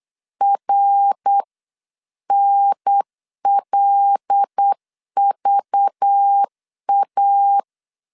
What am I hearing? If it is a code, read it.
Morse code, heard as a keyed tone, about 8 wpm, 800 Hz. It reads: RNLVA
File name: CW / DET / 1 6 1500C S B 8wpm 5142